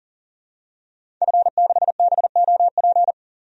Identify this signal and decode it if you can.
Morse code, heard as a keyed tone, about 40 wpm, 700 Hz. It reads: F6BOP
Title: DXCC-100 Call Signs - 5 Characters - 40wpm